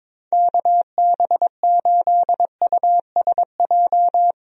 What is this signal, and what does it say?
Morse code, heard as a keyed tone, about 22 wpm, 700 Hz. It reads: KB8USJ
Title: USA Call Signs - 6 Characters - Repeated 3x - 22wpm